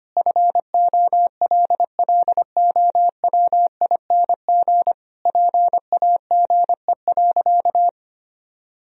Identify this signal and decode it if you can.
Morse code, heard as a keyed tone, about 25 wpm, 700 Hz. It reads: FOLLOWING PAGE.
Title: Sentences from Top 1000 Words - Review Part 1 - 25wpm